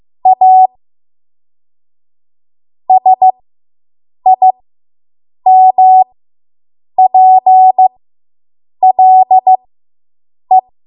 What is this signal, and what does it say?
Morse code, heard as a keyed tone, about 15 wpm, spaced as for 7 wpm, 750 Hz. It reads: A SIMPLE